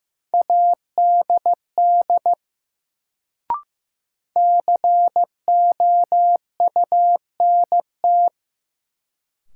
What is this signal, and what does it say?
Morse code, heard as a keyed tone, about 16 wpm, 700 Hz. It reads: ADD COUNT
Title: Sentences from Top 1500 Words - New Word Drill Alt - 15wpm